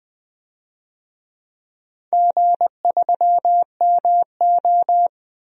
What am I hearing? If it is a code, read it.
Morse code, heard as a keyed tone, about 19 wpm, 700 Hz. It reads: G3MO
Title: DXCC-100 Call Signs - 4 Characters - Repeated 3x - 20wpm